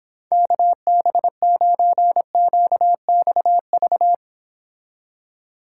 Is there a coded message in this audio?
Yes: Morse code from 0.3 to 4.1 s, heard as a keyed tone, about 31 wpm, 700 Hz. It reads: KB9QXV